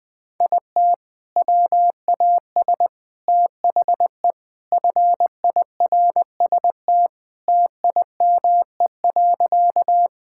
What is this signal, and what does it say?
Morse code, heard as a keyed tone, about 20 wpm, 700 Hz. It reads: IT WAS THE FIRST TIME.
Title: Sentences with 4-Word Phrases - 20wpm